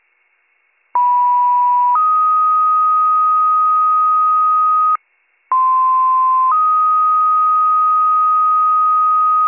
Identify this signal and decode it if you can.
Two-tone page, heard as a one-second tone, then a three-second tone, about 980 then 1250 Hz, about 1040 then 1250 Hz. It is heard from 0.9 to 9.5 s.